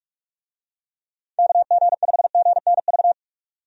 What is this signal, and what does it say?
Morse code, heard as a keyed tone, about 42 wpm, 700 Hz. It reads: KG5GNV
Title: USA Call Signs - 6 Characters - 45wpm